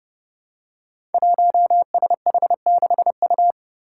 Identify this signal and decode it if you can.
Morse code, heard as a keyed tone, about 30 wpm, 700 Hz. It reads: 1SH6U
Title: Five Challenging Letter-Number Combinations 30wpm